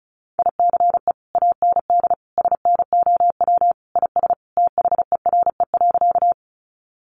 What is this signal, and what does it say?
Morse code, heard as a keyed tone, about 34 wpm, 700 Hz. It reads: ICE AND SNOW IS THERE.
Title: Sentences from Top 1000 Words 35wpm